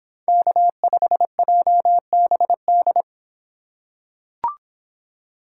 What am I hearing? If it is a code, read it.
Morse code, heard as a keyed tone, about 26 wpm, 700 Hz. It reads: K5JBD